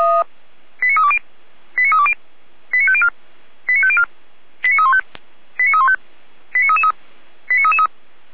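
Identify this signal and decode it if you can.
Five-tone selective call (ZVEI), heard as quick runs of five tones, tones of about 70 ms. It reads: 89320 89320 89504 89504 89315 89315 89333 89333
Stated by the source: Five-tone ZVEI selective call received by SDR, archived by Feuerwehr Orlamünde.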